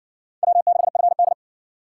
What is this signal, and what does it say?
Morse code, heard as a keyed tone, about 57 wpm, 690 Hz. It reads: W6FD